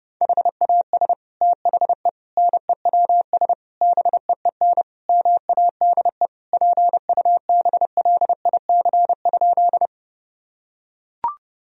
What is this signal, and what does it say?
Morse code, heard as a keyed tone, about 30 wpm, 700 Hz. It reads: HAS THE NEWS BEEN MADE PUBLIC?
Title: Sentences from Top 1000 Words - Review Part 3 - 30wpm